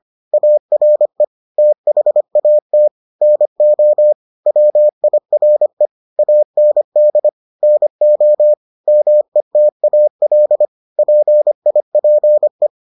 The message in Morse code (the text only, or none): ARE THAT NO WIRE AND NO METAL PIPE